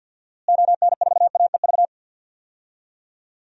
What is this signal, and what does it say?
Morse code, heard as a keyed tone, about 51 wpm, 700 Hz. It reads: ON4AEV
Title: DXCC-100 Call Signs - 6 Characters - Repeated 3x - 50wpm